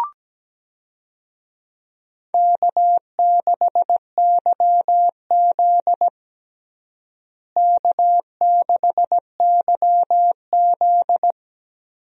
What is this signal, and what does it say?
Morse code, heard as a keyed tone, about 17 wpm, 700 Hz. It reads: K6YZ K6YZ
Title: USA Call Signs - 4 Characters - Repeated 3x - 17wpm